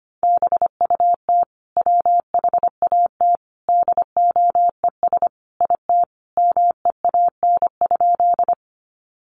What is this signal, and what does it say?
Morse code, heard as a keyed tone, about 25 wpm, 700 Hz. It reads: BUT WHAT DOES IT MEAN?
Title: Sentences from Top 200 Words 25wpm